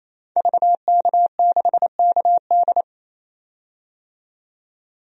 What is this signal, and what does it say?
Morse code, heard as a keyed tone, about 28 wpm, 700 Hz. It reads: VK6KD
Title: DXCC-100 Call Signs - 5 Characters - 28wpm